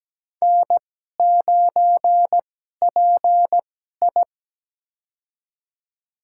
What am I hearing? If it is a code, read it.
Morse code, heard as a keyed tone, about 17 wpm, spaced as for 12 wpm, 700 Hz. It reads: N9PI
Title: USA Call Signs - 2x Character Spacing - 17wpm